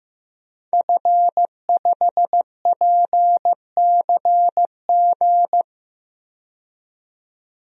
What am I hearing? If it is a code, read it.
Morse code, heard as a keyed tone, about 16 wpm, 700 Hz. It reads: F5PCG
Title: DXCC-100 Call Signs - 5 Characters - 15wpm